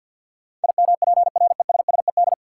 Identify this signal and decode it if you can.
Morse code, heard as a keyed tone, about 53 wpm, 700 Hz. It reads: IMPRESSED
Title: Sentences from Top 1600 Words - New Word Drill Alt - 50wpm